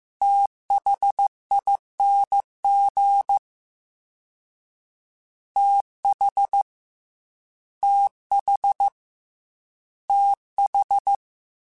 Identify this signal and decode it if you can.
Morse code, heard as a keyed tone, about 15 wpm, 800 Hz. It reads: THING TH TH TH
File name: Combinations / Combinations-mp3s / Combinations@13